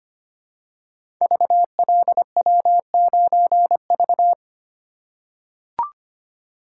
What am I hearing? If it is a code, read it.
Morse code, heard as a keyed tone, about 25 wpm, 700 Hz. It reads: VLW9V
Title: Code Groups - 5 Letter:Numbers - 25wpm